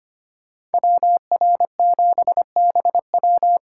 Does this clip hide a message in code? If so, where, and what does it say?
Yes: Morse code from 0.7 to 3.6 s, heard as a keyed tone, about 33 wpm, 700 Hz. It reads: WR7BW